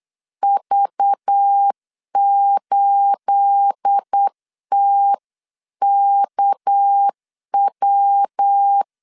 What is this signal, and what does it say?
Morse code, heard as a keyed tone, about 8 wpm, 800 Hz. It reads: V8TKW
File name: 8wpm